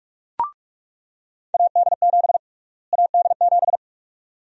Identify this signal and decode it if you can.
Morse code, heard as a keyed tone, about 41 wpm, 700 Hz. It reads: AD7 AD7
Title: USA Call Sign Prefixes - Repeated 2x - 45wpm